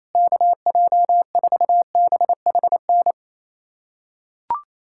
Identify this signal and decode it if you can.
Morse code, heard as a keyed tone, about 28 wpm, 700 Hz. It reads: KJ4BHN
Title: USA Callsign DE Callsign 28wpm